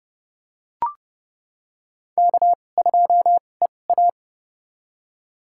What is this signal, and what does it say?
Morse code, heard as a keyed tone, about 31 wpm, spaced as for 22 wpm, 700 Hz. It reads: K2EA
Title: USA Call Signs - 4 Characters - 2x Character Spacing - 30wpm